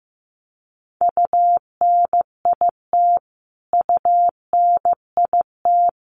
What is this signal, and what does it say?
Morse code, heard as a keyed tone, about 15 wpm, 700 Hz. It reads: UNIT UNIT
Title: Top 500 Words Repeated 15wpm